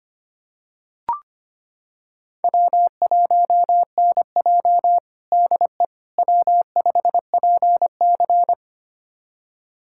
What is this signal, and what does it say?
Morse code, heard as a keyed tone, about 25 wpm, 700 Hz. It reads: W1NJ DE W5PC
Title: USA Callsign DE Callsign - 4 Characters - 25wpm